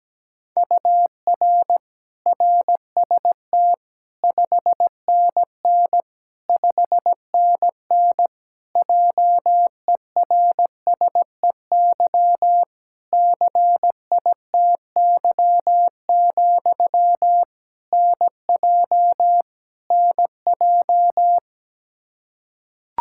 Morse code, heard as a keyed tone, about 17 wpm, 700 Hz. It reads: UR RST 5NN 5NN JERSEY CITY, NJ NJ
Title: Practice QSO - Exchange 1 - Beginning - 17wpm